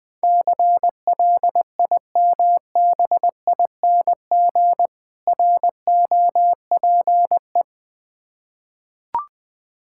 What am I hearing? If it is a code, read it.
Morse code, heard as a keyed tone, about 20 wpm, 700 Hz. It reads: CLIMBING ROPE
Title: Difficult Words with Hints - 20wpm